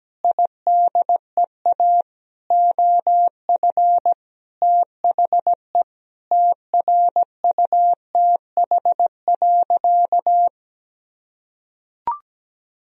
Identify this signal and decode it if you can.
Morse code, heard as a keyed tone, about 17 wpm, 700 Hz. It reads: IDEA OF THE TRUTH.
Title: Sentences from Top 1000 Words - Review Part 1 - 17wpm